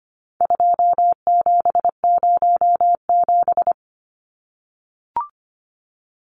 Morse code, heard as a keyed tone, about 25 wpm, 700 Hz. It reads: 2707
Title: Numbers - 4-Digit 25wpm